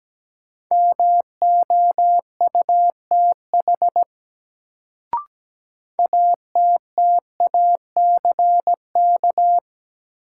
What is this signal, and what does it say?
Morse code, heard as a keyed tone, about 17 wpm, 700 Hz. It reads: MOUTH ATTACK
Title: Sentences from Top 1500 Words - New Word Drill - 17wpm